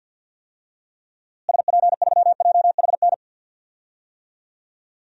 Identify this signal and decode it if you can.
Morse code, heard as a keyed tone, about 50 wpm, 700 Hz. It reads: SP3JHN
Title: DXCC-100 Call Signs - 6 Characters - Repeated 2x - 50wpm